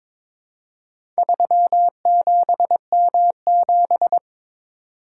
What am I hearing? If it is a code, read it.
Morse code, heard as a keyed tone, about 22 wpm, 690 Hz. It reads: 37M7
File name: practice / 22wpm / Four Letter-Number Combinations - 22wpm